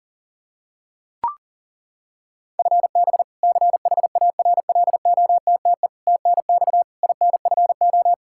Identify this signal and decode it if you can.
Morse code, heard as a keyed tone, about 40 wpm, 700 Hz. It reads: FB CHARLOTTE TNX INFO
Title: Practice QSO - Exchange 3 - More Info - 40wpm